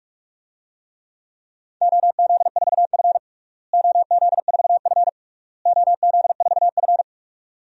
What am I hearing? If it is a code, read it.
Morse code, heard as a keyed tone, about 41 wpm, 700 Hz. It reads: OZ4F OZ4F OZ4F
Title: DXCC-100 Call Signs - 4 Characters - Repeated 3x - 45wpm